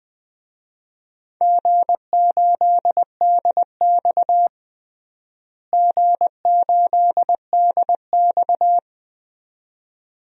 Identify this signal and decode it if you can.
Morse code, heard as a keyed tone, about 20 wpm, 700 Hz. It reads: G8DX G8DX
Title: DXCC-100 Call Signs - 4 Characters - Repeated 2x - 20wpm